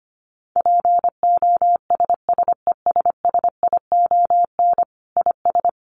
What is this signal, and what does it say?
Morse code, heard as a keyed tone, about 25 wpm, 700 Hz. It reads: POSSESSION IS
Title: Sentences from Top 1400 Words 25wpm